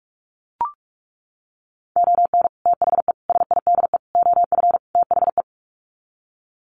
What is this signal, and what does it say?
Morse code, heard as a keyed tone, about 44 wpm, 700 Hz. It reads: ON THE SIDE OF THE
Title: Sets of 5 Words 45wpm